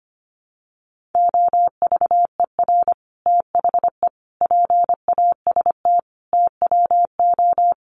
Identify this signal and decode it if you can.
Morse code, heard as a keyed tone, about 25 wpm, 700 Hz. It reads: OVER THE PAST TWO
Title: Sets of 5 Words Encore 25wpm